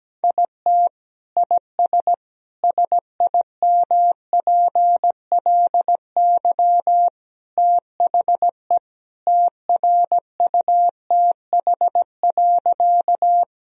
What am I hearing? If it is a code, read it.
Morse code, heard as a keyed tone, about 17 wpm, 700 Hz. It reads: IT IS SIMPLY THE TRUTH.
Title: Sentences from Top 1000 Words - Review Part 4 - 17wpm